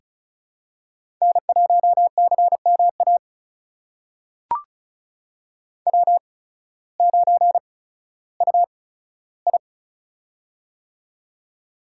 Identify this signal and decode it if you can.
Morse code, heard as a keyed tone, about 36 wpm, spaced as for 16 wpm, 700 Hz. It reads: N1CMA W9UI
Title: USA Call Signs - 8x Character Spacing - 35wpm